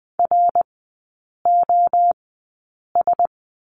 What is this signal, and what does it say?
Morse code, heard as a keyed tone, about 20 wpm, spaced as for 8 wpm, 700 Hz. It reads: ROS
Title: 053 Lesson 10e - Letter r - Three Characters 20wpm